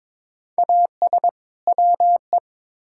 Morse code, heard as a keyed tone, about 22 wpm, 700 Hz. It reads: AS WE
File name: Build A Phrase - 2-Word Phrases - 22wpm